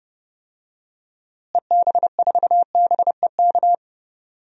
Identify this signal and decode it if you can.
Morse code, heard as a keyed tone, about 30 wpm, 700 Hz. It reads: EB4BEK
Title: DXCC-100 Call Signs - 6 Characters - Repeated 3x - 30wpm